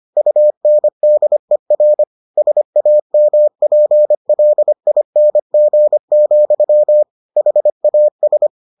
Morse code, heard as a keyed tone, about 25 wpm, 600 Hz. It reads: UNDER SAMPLING, HAS